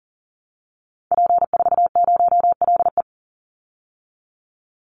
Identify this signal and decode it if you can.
Morse code, heard as a keyed tone, about 40 wpm, 700 Hz. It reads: P40LE
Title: Call Signs Repeated 3x - 40wpm